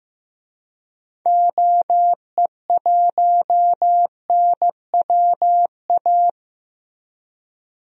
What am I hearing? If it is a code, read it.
Morse code, heard as a keyed tone, about 16 wpm, 700 Hz. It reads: OE1NWA
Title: DXCC-100 Call Signs - 6 Characters - Repeated 2x - 15wpm